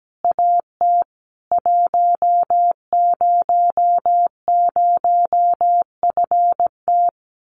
Morse code, heard as a keyed tone, about 17 wpm, 700 Hz. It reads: AT 100FT
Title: Common QSO Phrases 17wpm